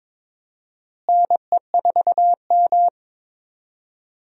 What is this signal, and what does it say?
Morse code, heard as a keyed tone, about 22 wpm, 700 Hz. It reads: NE4M
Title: USA Call Signs - 4 Characters - Repeated 3x - 22wpm